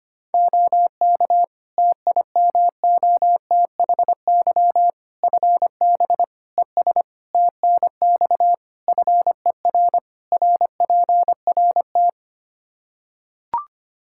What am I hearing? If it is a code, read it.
Morse code, heard as a keyed tone, about 25 wpm, 700 Hz. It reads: OK TIMOTHY FB ES TNX FER RPRT
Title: Practice QSO - Exchange 2 - Info - 25wpm